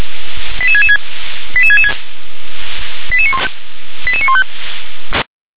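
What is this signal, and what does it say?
Five-tone selective call (ZVEI), heard as quick runs of five tones, tones of about 70 ms. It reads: 88666 88666 88915
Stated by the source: ZVEI five-tone selective call 88983 received on an SDR.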